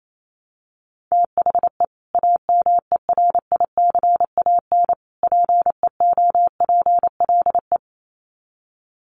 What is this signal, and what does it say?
Morse code, heard as a keyed tone, about 28 wpm, 700 Hz. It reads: THE AMERICAN PEOPLE
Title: Sets of 3 Words 28wpm